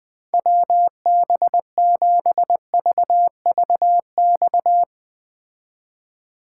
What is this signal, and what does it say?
Morse code, heard as a keyed tone, about 20 wpm, 700 Hz. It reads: WB7VVX